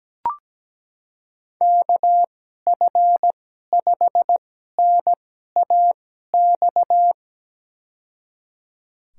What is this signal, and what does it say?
Morse code, heard as a keyed tone, about 17 wpm, spaced as for 12 wpm, 700 Hz. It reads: KF5NAX
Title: USA Call Signs - 6 Characters - 2x Character Spacing - 17wpm